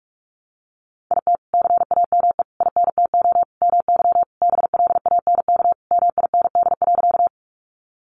Morse code, heard as a keyed tone, about 44 wpm, 700 Hz. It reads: IT CAME INTO MY BLANK MIND.
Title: Sentences from Top 1500 Words 45wpm